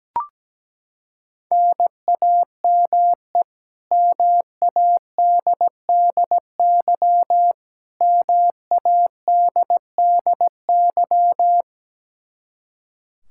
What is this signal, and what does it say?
Morse code, heard as a keyed tone, about 17 wpm, 700 Hz. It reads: NAME MADDY MADDY
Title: Practice QSO - Exchange 1 - Beginning - 17wpm